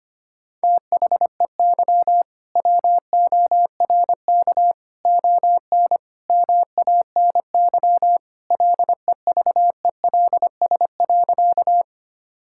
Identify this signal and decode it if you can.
Morse code, heard as a keyed tone, about 25 wpm, 700 Hz. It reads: THEY WORK ON MANY LEVELS.
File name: practice / 25wpm / All-Age Appropriate Jokes - 25wpm